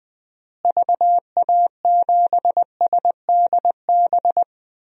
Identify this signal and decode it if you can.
Morse code, heard as a keyed tone, about 20 wpm, 700 Hz. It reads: VA7SDB
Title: DXCC-100 Call Signs - 6 Characters - Repeated 2x - 20wpm